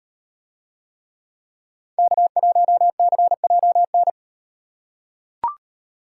Morse code, heard as a keyed tone, about 36 wpm, 700 Hz. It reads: K1CJN